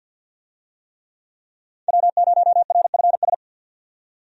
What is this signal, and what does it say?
Morse code, heard as a keyed tone, about 45 wpm, 700 Hz. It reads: W0RFS
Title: USA Call Signs - 5 Characters - 50wpm